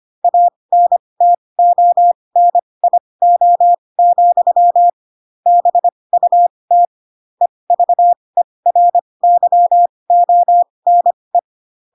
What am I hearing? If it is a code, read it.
Morse code, heard as a keyed tone, about 25 wpm, spaced as for 20 wpm, 700 Hz. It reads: ANTONIO, BUT EVERYONE
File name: Pinocchio / Pinocchio ch1 15 25 ik1hge